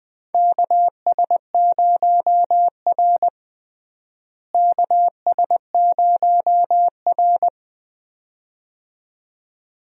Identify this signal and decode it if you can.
Morse code, heard as a keyed tone, about 21 wpm, 700 Hz. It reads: KS0R KS0R
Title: USA Call Signs - 4 Characters - Repeated 2x - 20wpm